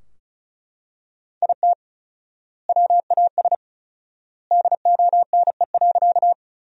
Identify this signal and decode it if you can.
Morse code, heard as a keyed tone, about 36 wpm, 700 Hz. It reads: IT WAS DONE.
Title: 4x Word-Spacing - Sentences from Top 500 Words - 35wpm